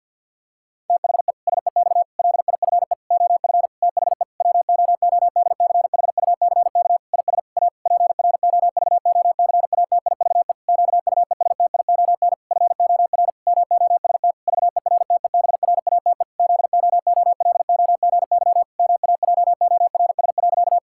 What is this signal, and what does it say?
Morse code, heard as a keyed tone, about 52 wpm, 700 Hz. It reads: THE SEX LIFE OF THE WOODCHUCK IS A PROVOCATIVE QUESTION FOR MOST VERTEBRATE ZOOLOGY MAJORS.
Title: Pangram Challenge v2 - 50wpm